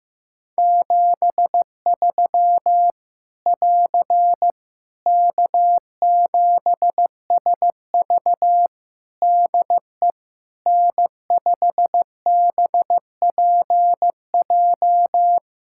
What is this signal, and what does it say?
Morse code, heard as a keyed tone, about 15 wpm, 700 Hz. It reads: 73 <AR> K7SV DE N5BPJ
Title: Practice QSO - Exchange 4 - Ending - 15wpm